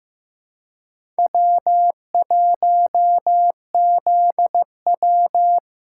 Morse code, heard as a keyed tone, about 15 wpm, 700 Hz. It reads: W1ZW
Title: USA Callsign DE Callsign - 4 Characters - 15wpm